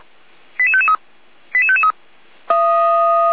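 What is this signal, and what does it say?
Five-tone selective call (ZVEI), heard as quick runs of five tones, tones of about 70 ms. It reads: 88553 88553
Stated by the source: Five-tone ZVEI selective call received by SDR.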